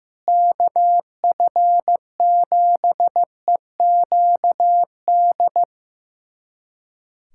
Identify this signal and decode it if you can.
Morse code, heard as a keyed tone, about 16 wpm, 700 Hz. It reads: KF7EQD
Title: USA Callsign DE Callsign - 6 Characters - 15wpm